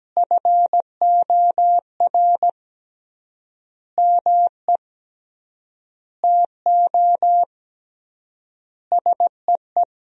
Morse code, heard as a keyed tone, about 17 wpm, 700 Hz. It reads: FOR ME TO SEE
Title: 3x Word-Spacing - Sentences from Top 500 Words - 17wpm